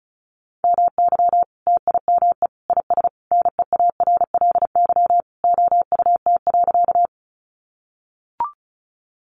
Morse code, heard as a keyed tone, about 34 wpm, 700 Hz. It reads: MY TIME IS NEARLY OUT.
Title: Sentences from Top 1300 Words 35wpm